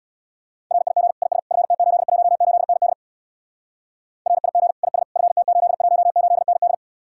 Morse code, heard as a keyed tone, about 61 wpm, 700 Hz. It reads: RIG IS FT818ND RIG IS FT818ND
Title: Speed-Racing Common QSO Phrases - 50wpm